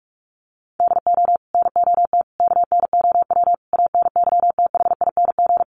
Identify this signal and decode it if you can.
Morse code, heard as a keyed tone, about 44 wpm, 700 Hz. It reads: DO NOT KNOW ANYTHING
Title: Sets of 5 Words 45wpm